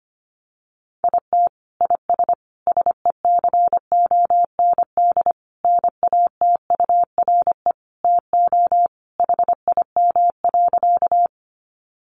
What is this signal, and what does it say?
Morse code, heard as a keyed tone, about 25 wpm, 700 Hz. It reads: IT IS SECOND NATURE TO HIM.
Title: Sentences from Top 600 Words 25wpm